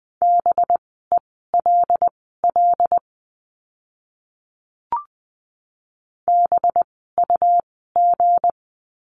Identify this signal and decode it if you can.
Morse code, heard as a keyed tone, about 20 wpm, spaced as for 14 wpm, 700 Hz. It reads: BELL BUG
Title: 185 Lesson 30f - Letter b - Words 20wpm